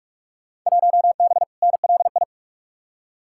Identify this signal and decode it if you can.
Morse code, heard as a keyed tone, about 46 wpm, 700 Hz. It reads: 1B NLI
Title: Field Day Exchanges - 45wpm